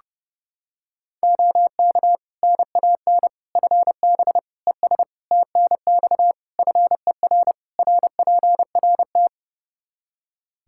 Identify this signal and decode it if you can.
Morse code, heard as a keyed tone, about 30 wpm, 700 Hz. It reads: OK NAN FB ES TNX FER RPRT
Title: Practice QSO - Exchange 2 - Info - 30wpm